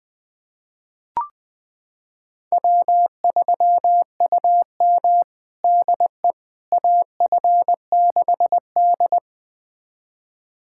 Morse code, heard as a keyed tone, about 21 wpm, 700 Hz. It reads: W3UM DE AF6D
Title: USA Callsign DE Callsign - 4 Characters - 20wpm